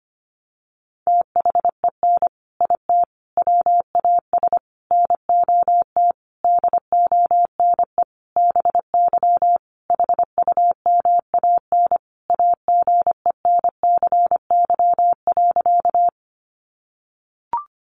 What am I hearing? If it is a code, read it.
Morse code, heard as a keyed tone, about 25 wpm, 700 Hz. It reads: THEN IT WAS NOT DONE BY HUMAN AGENCY.
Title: Sentences from Top 1300 Words 25wpm